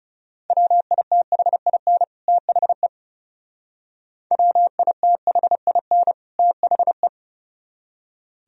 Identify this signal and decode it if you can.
Morse code, heard as a keyed tone, about 36 wpm, 700 Hz. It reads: WITHIN THE WITHIN THE
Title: Speed-Racing - Sets of 2 Words 30wpm